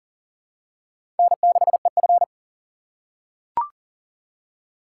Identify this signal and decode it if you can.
Morse code, heard as a keyed tone, about 38 wpm, 700 Hz. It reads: N6EF
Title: USA Call Signs - 4 Characters - 6x Character Spacing - 40wpm